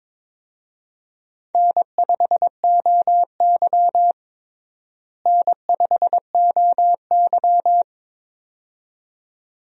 Morse code, heard as a keyed tone, about 22 wpm, 700 Hz. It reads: N5OY N5OY
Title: USA Call Signs - 4 Characters - Repeated 2x - 22wpm